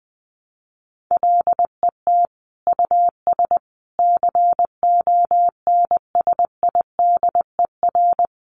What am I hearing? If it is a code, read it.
Morse code, heard as a keyed tone, about 20 wpm, 700 Hz. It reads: LET US CONSIDER
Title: Sentences from Top 600 Words 20wpm